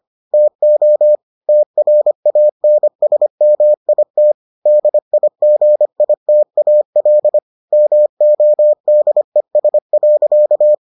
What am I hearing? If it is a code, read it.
Morse code, heard as a keyed tone, about 25 wpm, 600 Hz. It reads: TO TRANSMIT DIGITAL MODES.